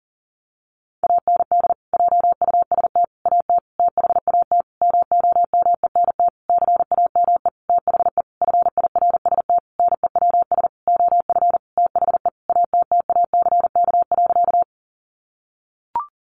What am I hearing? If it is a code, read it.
Morse code, heard as a keyed tone, about 40 wpm, 700 Hz. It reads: AND JUST AT THAT MOMENT CAME THE FIRST NEWS OF THE ATTACK.
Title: Sentences from Top 1500 Words 40wpm